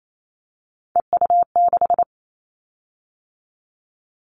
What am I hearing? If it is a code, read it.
Morse code, heard as a keyed tone, about 28 wpm, 700 Hz. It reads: EU6
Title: Callsign Prefixes for Top 50 DXCC - 28wpm